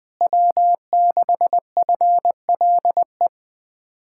Morse code, heard as a keyed tone, about 20 wpm, 700 Hz. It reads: W6FLE
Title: Speed-Racing Common QSO Phrases - 20wpm